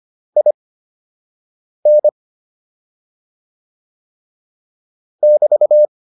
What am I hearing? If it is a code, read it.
Morse code, heard as a keyed tone, about 25 wpm, spaced as for 6 wpm, 600 Hz. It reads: IN <BT>